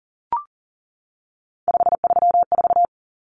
Morse code, heard as a keyed tone, about 40 wpm, 700 Hz. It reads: 534
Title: Common Cut Numbers - 3-Digit 40wpm